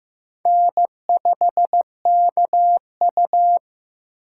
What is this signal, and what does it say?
Morse code, heard as a keyed tone, about 18 wpm, 700 Hz. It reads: N5KU
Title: Speed Racing - USA Call Signs 15wpm